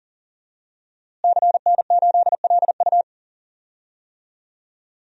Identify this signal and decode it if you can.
Morse code, heard as a keyed tone, about 40 wpm, 700 Hz. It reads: CN8LU
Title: DXCC-100 Call Signs - 5 Characters - Repeated 2x - 40wpm